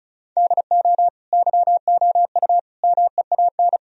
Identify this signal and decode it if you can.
Morse code, heard as a keyed tone, about 34 wpm, 700 Hz. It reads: DO YOU MEAN
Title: Sentences from Top 1000 Words - Review Part 4 - 35wpm